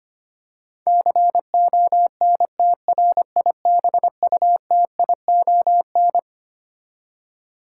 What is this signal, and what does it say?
Morse code, heard as a keyed tone, about 25 wpm, 700 Hz. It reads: CONTRIBUTION
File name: Twelve Letter Words - 25wpm